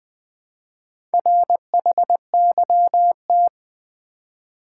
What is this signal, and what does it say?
Morse code, heard as a keyed tone, about 20 wpm, 700 Hz. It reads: RHYT
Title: Build A Word - 2 Letters At A Time - 6-Letter Words - 20wpm